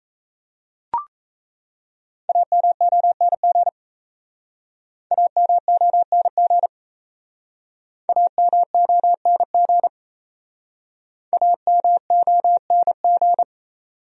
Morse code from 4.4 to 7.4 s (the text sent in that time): AMONG